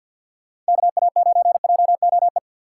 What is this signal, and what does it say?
Morse code, heard as a keyed tone, about 52 wpm, 700 Hz. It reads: KA9JOE
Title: USA Call Signs - 4x Character Spacing - 50wpm